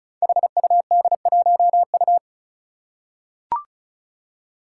Morse code, heard as a keyed tone, about 36 wpm, 700 Hz. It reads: HUD1U
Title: Five Challenging Letter-Number Combinations 35wpm